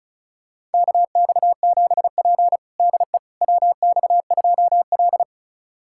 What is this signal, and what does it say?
Morse code, heard as a keyed tone, about 36 wpm, 700 Hz. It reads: KX7P DE WX2L
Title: USA Callsign DE Callsign - 4 Characters - 35wpm